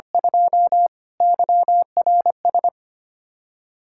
Morse code, heard as a keyed tone, about 25 wpm, 700 Hz. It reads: 2 YRS
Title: Practice QSO - Exchange 3 - More Info - 25wpm